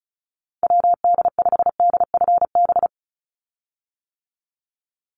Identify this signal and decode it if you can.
Morse code, heard as a keyed tone, about 34 wpm, 700 Hz. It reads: WD5DFB
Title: USA Call Signs 35wpm